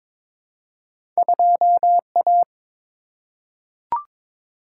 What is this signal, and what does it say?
Morse code, heard as a keyed tone, about 22 wpm, 700 Hz. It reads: 2A
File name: practice / 22wpm / Two Letter-Number Combinations - 22wpm